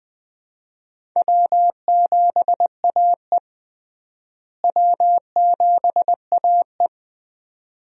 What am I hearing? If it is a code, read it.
Morse code, heard as a keyed tone, about 20 wpm, 700 Hz. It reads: W7AE W7AE
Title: USA Call Signs - 4 Characters - Repeated 2x - 20wpm